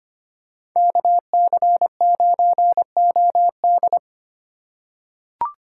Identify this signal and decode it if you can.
Morse code, heard as a keyed tone, about 25 wpm, 700 Hz. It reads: KC9OD
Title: USA Call Signs - 5 Characters - 3x Character Spacing - 25wpm